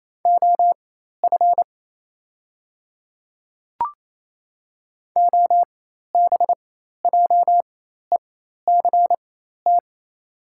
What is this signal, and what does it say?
Morse code, heard as a keyed tone, about 28 wpm, spaced as for 13 wpm, 700 Hz. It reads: OF OBJECT
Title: 4x Character Spacing - Top 500 Words - 28wpm